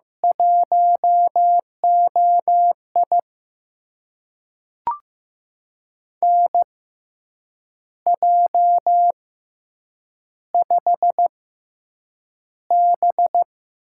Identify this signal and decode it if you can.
Morse code, heard as a keyed tone, about 15 wpm, spaced as for 7 wpm, 700 Hz. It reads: 1OI NJ5B
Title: USA Call Signs - 4 Characters - 6x Character Spacing - 15wpm